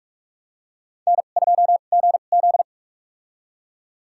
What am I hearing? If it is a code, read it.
Morse code, heard as a keyed tone, about 44 wpm, spaced as for 32 wpm, 700 Hz. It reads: N2GZ
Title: USA Call Signs - 4 Characters - 2x Character Spacing - 45wpm